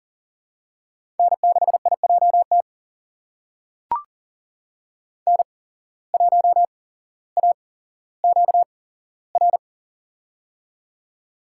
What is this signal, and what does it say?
Morse code, heard as a keyed tone, about 40 wpm, spaced as for 17 wpm, 700 Hz. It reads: N6IJT N1AQR
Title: USA Call Signs - 5 Characters - 8x Character Spacing - 40wpm